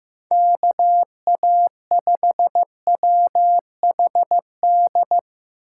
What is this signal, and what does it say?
Morse code, heard as a keyed tone, about 15 wpm, 700 Hz. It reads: KA5WHD
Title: USA Call Signs - 6 Characters - 6x Character Spacing - 15wpm